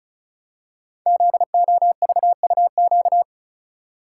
Morse code, heard as a keyed tone, about 36 wpm, 700 Hz. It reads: ZOVUQ
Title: Code Groups - 5 Letters - 35wpm